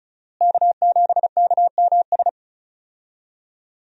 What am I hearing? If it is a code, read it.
Morse code, heard as a keyed tone, about 36 wpm, 700 Hz. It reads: K7KMS
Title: USA Call Signs - 5 Characters - Repeated 3x - 35wpm